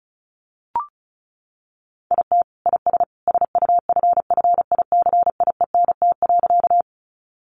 Morse code, heard as a keyed tone, about 34 wpm, 700 Hz. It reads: IT IS SUFFICIENT.
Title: Sentences from Top 1400 Words 35wpm